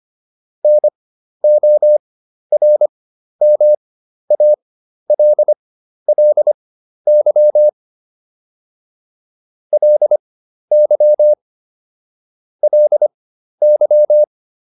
Morse code, heard as a keyed tone, about 25 wpm, spaced as for 12 wpm, 600 Hz. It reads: NORMALLY LY LY